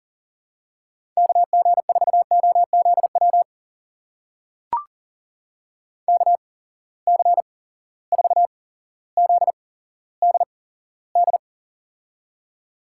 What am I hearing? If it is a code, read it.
Morse code, heard as a keyed tone, about 41 wpm, spaced as for 17 wpm, 700 Hz. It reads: KG4OZW KC4ZDD
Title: USA Call Signs - 6 Characters - 8x Character Spacing - 40wpm